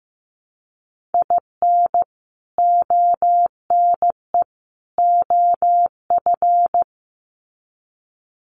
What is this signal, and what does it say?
Morse code, heard as a keyed tone, about 15 wpm, 700 Hz. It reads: IN ONE OF
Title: Sets of 3 Words 15wpm